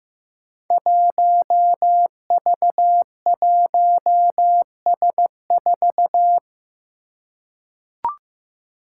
Morse code, heard as a keyed tone, about 15 wpm, 700 Hz. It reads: 1V1S4
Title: Five Challenging Letter-Number Combinations 15wpm